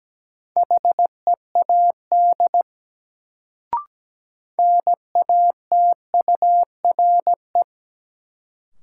Morse code, heard as a keyed tone, about 17 wpm, 700 Hz. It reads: HEAD NATURE
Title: Sets of 3 Words - New Word Drill - 17wpm